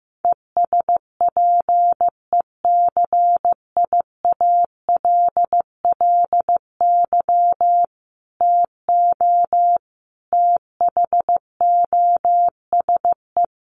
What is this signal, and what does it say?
Morse code, heard as a keyed tone, about 15 wpm, 700 Hz. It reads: ESPECIALLY TO THOSE
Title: Sentences from Top 1500 Words 15wpm